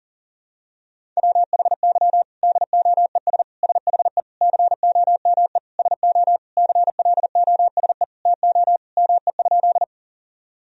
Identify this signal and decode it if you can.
Morse code, heard as a keyed tone, about 40 wpm, 700 Hz. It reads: WHY DOES SHE COME SO CLOSE TO ME?
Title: Sentences from Top 1000 Words - Review Part 1 - 40wpm